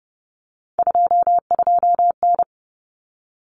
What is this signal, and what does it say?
Morse code, heard as a keyed tone, about 30 wpm, 700 Hz. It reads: 22N
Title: Common Cut Numbers - 3-Digit 30wpm